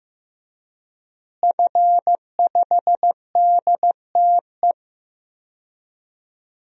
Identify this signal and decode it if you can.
Morse code, heard as a keyed tone, about 15 wpm, 700 Hz. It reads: F5DTE
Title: DXCC-100 Call Signs - 5 Characters - Repeated 2x - 15wpm